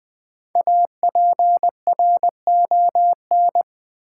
Morse code, heard as a keyed tone, about 20 wpm, 700 Hz. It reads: APRON
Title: Easy Words with Hints - 20wpm